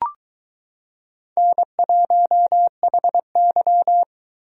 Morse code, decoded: N1HY

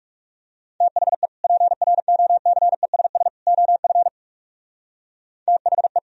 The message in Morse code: THE PROCESS OF THE

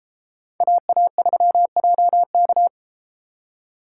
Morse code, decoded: AA3JK

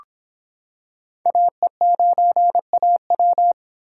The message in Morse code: AE9AW

